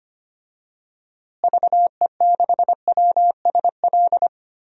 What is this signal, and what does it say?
Morse code, heard as a keyed tone, about 25 wpm, 700 Hz. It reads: VE6WSL
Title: DXCC-100 Call Signs - 6 Characters - Repeated 2x - 25wpm